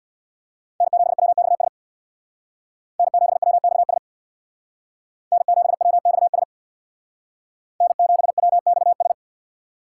Morse code, heard as a keyed tone, about 64 wpm, 700 Hz. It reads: N7WXS N7WXS N7WXS N7WXS